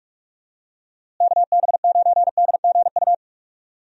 Morse code, heard as a keyed tone, about 46 wpm, 700 Hz. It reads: KB9DGU
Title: USA Call Signs - 6 Characters - Repeated 3x - 45wpm